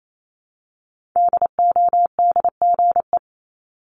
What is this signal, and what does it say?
Morse code, heard as a keyed tone, about 28 wpm, 700 Hz. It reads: DODGE
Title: Multi-letter Phonemes Repeated 3x 28wpm